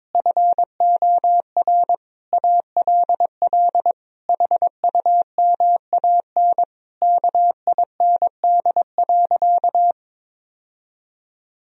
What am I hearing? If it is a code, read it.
Morse code, heard as a keyed tone, about 22 wpm, 700 Hz. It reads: FOR ALL HUMAN KIND.
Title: Sentences from Top 1000 Words - Review Part 4 - 22wpm